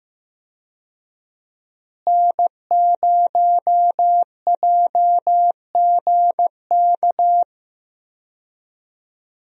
Morse code, heard as a keyed tone, about 15 wpm, 700 Hz. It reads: N0JGK
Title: USA Call Signs - 5 Characters - Repeated 2x - 15wpm